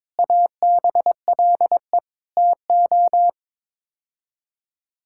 Morse code, heard as a keyed tone, about 22 wpm, 700 Hz. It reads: ABLE TO
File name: practice / 22wpm / Sets of 7 Words - 22wpm